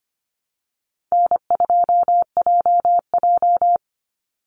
Morse code, heard as a keyed tone, about 25 wpm, 700 Hz. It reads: N2JJ
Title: Call Signs Repeated 3x - 25wpm